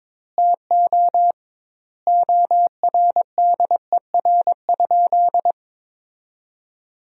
Morse code, heard as a keyed tone, about 22 wpm, 700 Hz. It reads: TO ORDER?
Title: 2x Word-Spacing - Sentences from Top 400 Words - 22wpm